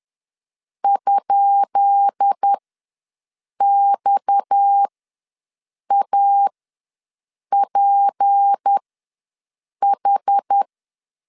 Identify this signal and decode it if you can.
Morse code, heard as a keyed tone, about 10 wpm, spaced as for 6 wpm, 800 Hz. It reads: ?XAPH
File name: LEZIONI / 10 SIGNS / 10B INT ws 10wpm 4134